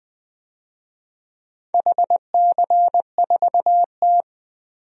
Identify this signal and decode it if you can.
Morse code, heard as a keyed tone, about 20 wpm, 700 Hz. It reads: HC4T
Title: DXCC-100 Call Signs - 4 Characters - 20wpm